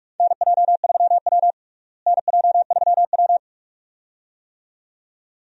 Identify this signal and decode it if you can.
Morse code, heard as a keyed tone, about 45 wpm, 700 Hz. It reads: NJ3W NJ3W
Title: USA Call Signs - 4 Characters - Repeated 2x - 45wpm